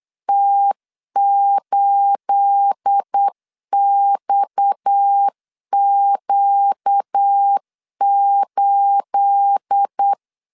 Morse code, heard as a keyed tone, about 8 wpm, 800 Hz. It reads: T8XQ8
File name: CW / DET / 1 9 1500C S C 8wpm 5653